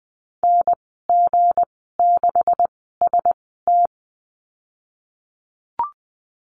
Morse code, heard as a keyed tone, about 20 wpm, spaced as for 15 wpm, 700 Hz. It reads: NG6ST
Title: 167 Lesson 27g - Number 7 - Call Signs 20wpm